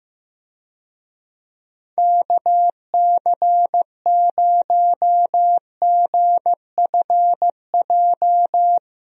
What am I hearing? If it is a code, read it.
Morse code, heard as a keyed tone, about 15 wpm, 700 Hz. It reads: KC0GFJ
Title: USA Call Signs - 6 Characters - 15wpm